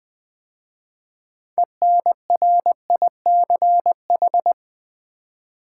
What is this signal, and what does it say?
Morse code, heard as a keyed tone, about 20 wpm, 700 Hz. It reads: ENRICH
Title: 6 Letter Words - First 3 Letters Given - 20wpm